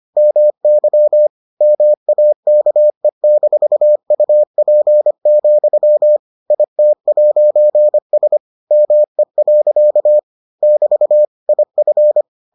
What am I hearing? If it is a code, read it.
Morse code, heard as a keyed tone, about 25 wpm, 600 Hz. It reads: MY MAKE-UP, IT'S ME. <BT> IF